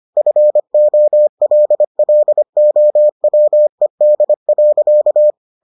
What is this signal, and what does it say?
Morse code, heard as a keyed tone, about 25 wpm, 600 Hz. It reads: FOLLOWED.